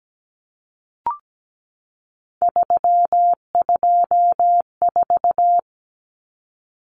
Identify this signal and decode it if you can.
Morse code, heard as a keyed tone, about 17 wpm, 700 Hz. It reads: 324
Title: Numbers - 3-Digit 17wpm